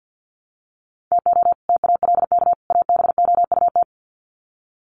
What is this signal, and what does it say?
Morse code, heard as a keyed tone, about 50 wpm, 700 Hz. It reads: TO TALK ABOUT
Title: Sets of 3 Words 50wpm